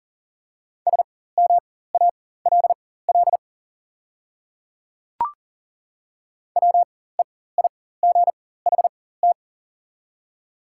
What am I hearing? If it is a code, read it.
Morse code, heard as a keyed tone, about 40 wpm, spaced as for 19 wpm, 690 Hz. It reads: SMALL WEIGHT
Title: 4x Character Spacing - Top 500 Words - 40wpm